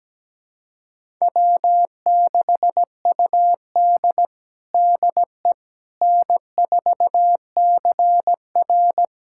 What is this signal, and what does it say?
Morse code, heard as a keyed tone, about 17 wpm, 700 Hz. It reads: W6UD DE N4CR
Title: USA Callsign DE Callsign - 4 Characters - 17wpm